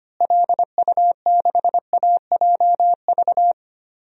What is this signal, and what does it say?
Morse code, heard as a keyed tone, about 25 wpm, 700 Hz. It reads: LU6AJV
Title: DXCC-100 Call Signs - 6 Characters - Repeated 2x - 25wpm